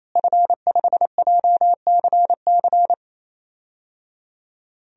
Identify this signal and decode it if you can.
Morse code, heard as a keyed tone, about 27 wpm, 700 Hz. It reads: F5JCC
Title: DXCC-100 Call Signs - 5 Characters - Repeated 3x - 28wpm